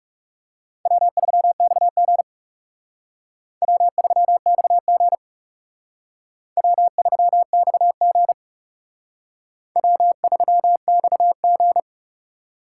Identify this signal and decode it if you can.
Morse code, heard as a keyed tone, about 37 wpm, 700 Hz. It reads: W3XG W3XG W3XG W3XG
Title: Speed Racing - USA 1x2 Call Signs 30wpm